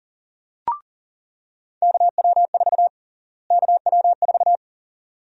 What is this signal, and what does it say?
Morse code, heard as a keyed tone, about 38 wpm, 700 Hz. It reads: KW4 KW4
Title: USA Call Sign Prefixes - Repeated 2x - 40wpm